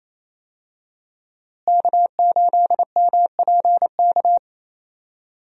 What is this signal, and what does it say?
Morse code, heard as a keyed tone, about 28 wpm, 700 Hz. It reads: K8MPK
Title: USA Call Signs - 5 Characters - Repeated 2x - 28wpm